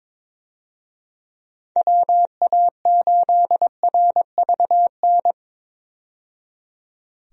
Morse code, heard as a keyed tone, about 22 wpm, 700 Hz. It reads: WA8RVN
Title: USA Call Signs - 6 Characters - 22wpm